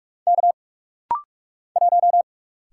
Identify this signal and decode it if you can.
Morse code, heard as a keyed tone, about 44 wpm, 700 Hz. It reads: K 1
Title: Single Letter-Number Alt - Rapid-Fire 45wpm